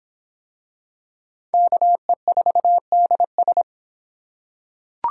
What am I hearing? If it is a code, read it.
Morse code, heard as a keyed tone, about 22 wpm, 700 Hz. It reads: KE4DS